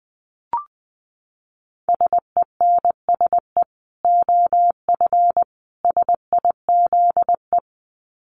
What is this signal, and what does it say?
Morse code, heard as a keyed tone, about 20 wpm, 700 Hz. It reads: SENSE OF SIZE
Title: Dit Challenge 20wpm